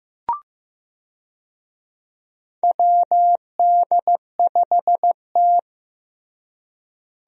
Morse code, heard as a keyed tone, about 15 wpm, 700 Hz. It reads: WD5T
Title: USA Call Signs - 4 Characters - Repeated 2x - 15wpm